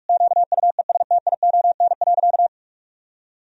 Morse code, heard as a keyed tone, about 47 wpm, 700 Hz. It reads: QUESTION.
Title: Sentences from Top 1000 Words - Review Part 3 - 45wpm